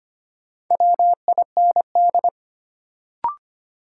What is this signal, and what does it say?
Morse code, heard as a keyed tone, about 25 wpm, 700 Hz. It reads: WIND
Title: Sentences from Top 400 Words - New Word Drill - 25wpm